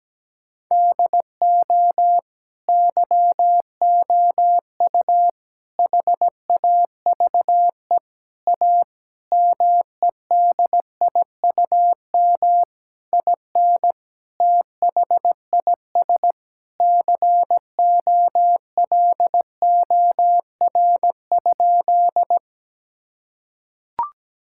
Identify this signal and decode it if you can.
Morse code, heard as a keyed tone, about 17 wpm, 700 Hz. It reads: DO YOU HAVE A MEDIUM IN THIS COLOR?
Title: Sentences from Top 1000 Words - Review Part 6 - 17wpm